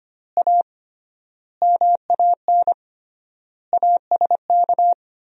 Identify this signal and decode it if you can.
Morse code, heard as a keyed tone, about 25 wpm, 700 Hz. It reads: A MAN ASK
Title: 3x Word-Spacing - Sentences from Top 200 Words - 25wpm